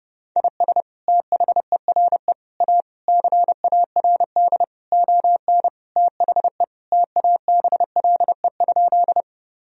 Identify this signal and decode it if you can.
Morse code, heard as a keyed tone, about 30 wpm, 700 Hz. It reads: IS THERE A CARD ON THE TABLE?
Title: Sentences from Top 1000 Words - Review Part 4 - 30wpm